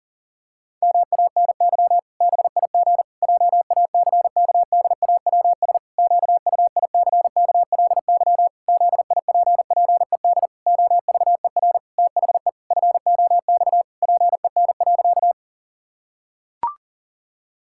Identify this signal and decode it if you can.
Morse code, heard as a keyed tone, about 40 wpm, 700 Hz. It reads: MANY BIG JACKDAWS QUICKLY ZIPPED OVER THE FOX PEN.
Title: Pangram Challenge v2 - 40wpm